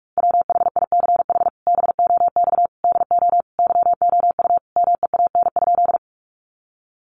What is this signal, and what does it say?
Morse code, heard as a keyed tone, about 44 wpm, 700 Hz. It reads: WHICH BOX DO YOU MEAN?
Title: Sentences from Top 500 Words 45wpm